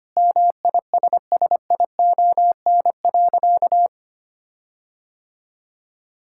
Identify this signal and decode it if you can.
Morse code, heard as a keyed tone, about 25 wpm, 700 Hz. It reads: MISSION.
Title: Sentences from Top 1600 Words 25wpm